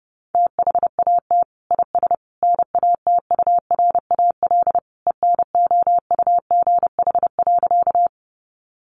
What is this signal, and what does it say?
Morse code, heard as a keyed tone, about 28 wpm, 700 Hz. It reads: THAT IS NATURAL ENOUGH.
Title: Sentences from Top 700 Words 30wpm